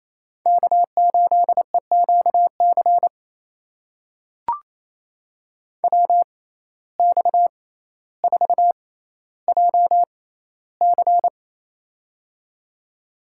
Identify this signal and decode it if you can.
Morse code, heard as a keyed tone, about 28 wpm, spaced as for 14 wpm, 700 Hz. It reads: K8EQC WX4JC
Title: USA Call Signs - 5 Characters - 6x Character Spacing - 28wpm